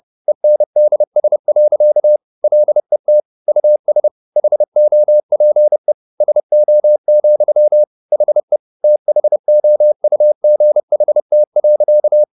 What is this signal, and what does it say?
Morse code, heard as a keyed tone, about 30 wpm, 600 Hz. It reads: ENDS. LET US HOPE SO, HE THOUGHT.
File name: the old man and the sea / the old man and the sea 1 30wpm